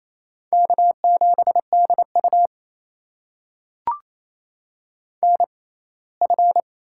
Morse code, heard as a keyed tone, about 28 wpm, spaced as for 19 wpm, 700 Hz. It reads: K7DU NF
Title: USA Call Signs - 4 Characters - 6x Character Spacing - 28wpm